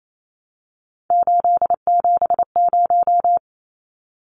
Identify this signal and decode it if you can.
Morse code, heard as a keyed tone, about 28 wpm, 700 Hz. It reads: 870
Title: Numbers - 3-Digit 28wpm